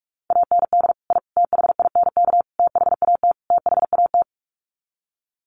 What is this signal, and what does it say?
Morse code, heard as a keyed tone, about 44 wpm, 700 Hz. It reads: AND I THINK THAT THAT
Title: Sets of 5 Words Encore 45wpm